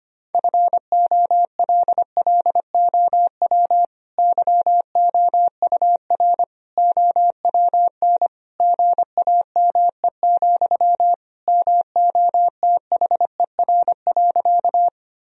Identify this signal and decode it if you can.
Morse code, heard as a keyed tone, about 25 wpm, 700 Hz. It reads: FOLLOW YOUR OWN GAME, MOTHER.
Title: Sentences from Top 1000 Words - Review Part 3 - 25wpm